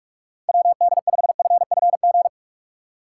Morse code, heard as a keyed tone, about 42 wpm, 700 Hz. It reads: WD5FFG
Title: USA Call Signs - 6 Characters - 3x Character Spacing - 45wpm